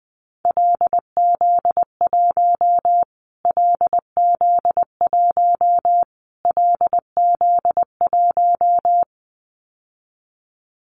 Morse code, heard as a keyed tone, about 20 wpm, 700 Hz. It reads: LZ1 LZ1 LZ1
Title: Callsign Prefixes for Top 25 DXCC - Repeated 3x - 20wpm